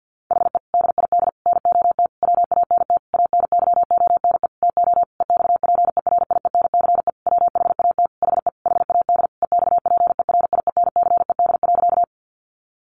Morse code, heard as a keyed tone, about 50 wpm, 700 Hz. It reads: HE DID NOT WANT ANYONE TO EXPERIENCE WHAT HE HAD EXPERIENCED.
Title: Sentences from Top 1100 Words 50wpm